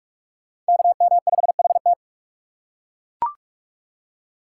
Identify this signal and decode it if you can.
Morse code, heard as a keyed tone, about 41 wpm, 700 Hz. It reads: KM5HT